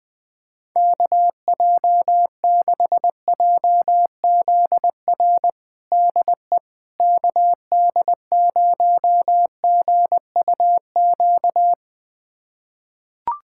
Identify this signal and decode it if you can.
Morse code, heard as a keyed tone, about 21 wpm, 700 Hz. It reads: KJ6JZR DE KD0GUQ
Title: USA Callsign DE Callsign 20wpm